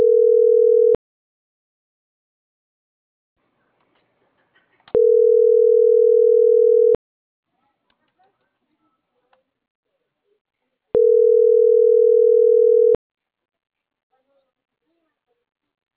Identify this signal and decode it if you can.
Ringback tone (North America), 440 + 480 Hz.